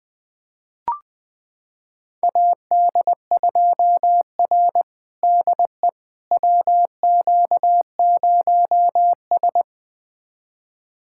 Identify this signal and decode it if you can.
Morse code, heard as a keyed tone, about 20 wpm, 700 Hz. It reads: AD2R DE WQ0S
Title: USA Callsign DE Callsign - 4 Characters - 20wpm